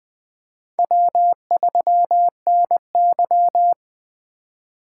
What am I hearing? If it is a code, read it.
Morse code, heard as a keyed tone, about 20 wpm, 700 Hz. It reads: W3NY
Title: 2023 POTA Activator Call Signs - 20wpm